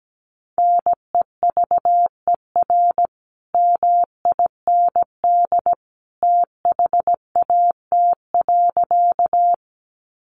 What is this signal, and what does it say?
Morse code, heard as a keyed tone, about 17 wpm, 700 Hz. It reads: NEVER MIND THAT.
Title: Sentences from Top 500 Words 17wpm